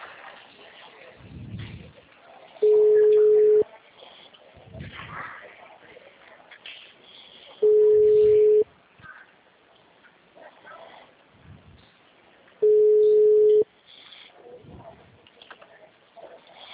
Ringback tone (Europe, 425 Hz).
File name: out- -2001- - - .14